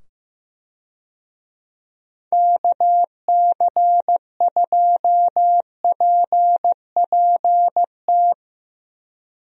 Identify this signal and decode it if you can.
Morse code, heard as a keyed tone, about 15 wpm, 700 Hz. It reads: KC2PPT
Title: USA Call Signs - 6 Characters - 15wpm